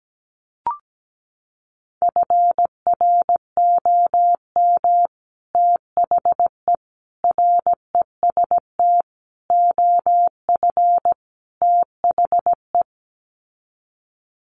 Morse code, heard as a keyed tone, about 17 wpm, 700 Hz. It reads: FROM THE REST OF THE
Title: Sets of 5 Words 17wpm